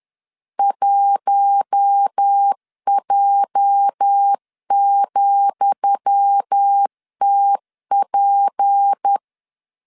Morse code, heard as a keyed tone, about 10 wpm, 800 Hz. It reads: 1J,TP
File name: LEZIONI / 10 SIGNS / 10D COMA 10wpm 3013